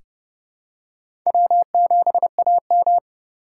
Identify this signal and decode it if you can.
Morse code, heard as a keyed tone, about 32 wpm, 700 Hz. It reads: W7AM
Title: USA Call Signs - 4 Characters - 2x Character Spacing - 30wpm